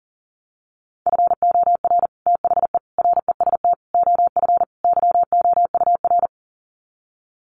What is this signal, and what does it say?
Morse code, heard as a keyed tone, about 38 wpm, 700 Hz. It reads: FOR THE REST OF YOUR
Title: Sets of 5 Words Encore 40wpm